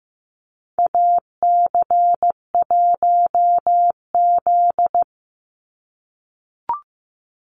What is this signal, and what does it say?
Morse code, heard as a keyed tone, about 15 wpm, 700 Hz. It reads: AC1Z
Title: NA SOTA Activator Callsigns 15wpm